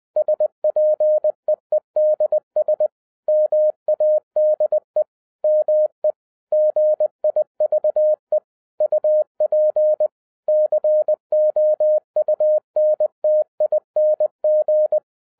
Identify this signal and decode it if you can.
Morse code, heard as a keyed tone, about 20 wpm, 610 Hz. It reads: SPEEDS MADE ME GIVE UP COUNTING
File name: 2011-10-mp3 / n13-TIPS FROM MY 34 YEARS OF EXPERIENCE 20wpm